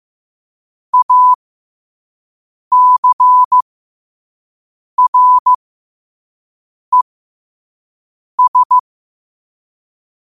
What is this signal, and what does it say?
Morse code, heard as a keyed tone, about 15 wpm, spaced as for 5 wpm, 1000 Hz. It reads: ACRES
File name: qotdfiles / 5wpm